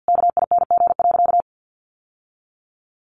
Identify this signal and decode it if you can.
Morse code, heard as a keyed tone, about 50 wpm, 700 Hz. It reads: KING.
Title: Sentences from Top 600 Words 50wpm